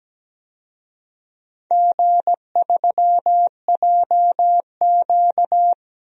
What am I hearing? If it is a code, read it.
Morse code, heard as a keyed tone, about 17 wpm, 700 Hz. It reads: G3JQ
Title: DXCC-100 Call Signs - 4 Characters - 17wpm